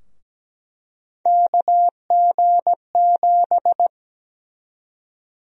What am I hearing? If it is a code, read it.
Morse code, heard as a keyed tone, about 17 wpm, 700 Hz. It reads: KG7
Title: USA Call Sign Prefixes - 17wpm